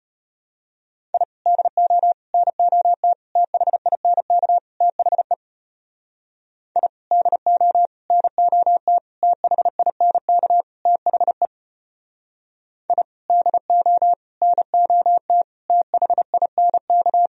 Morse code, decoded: I DO NOT THINK THE I DO NOT THINK THE I DO NOT THINK

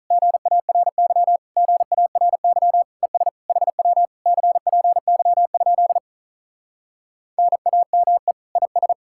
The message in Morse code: GARY GARY ES HW CPY? NAME IS

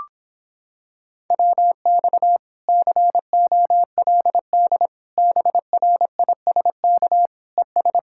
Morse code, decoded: WX COLD BRISK ES

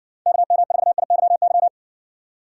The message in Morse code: KM4IYY

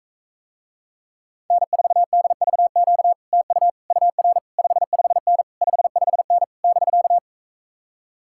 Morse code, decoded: N4DVQ TU UR 55N 55N <BK>

